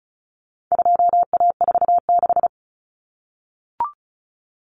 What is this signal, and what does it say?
Morse code, heard as a keyed tone, about 34 wpm, 700 Hz. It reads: 2A46
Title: Common Cut Numbers - 4-Digit 35wpm